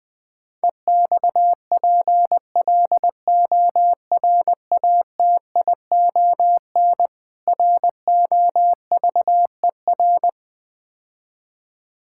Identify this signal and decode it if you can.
Morse code, heard as a keyed tone, about 20 wpm, 700 Hz. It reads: EXPLORATION ROVER
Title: Difficult Words - Speed-Racing with Hints - 20wpm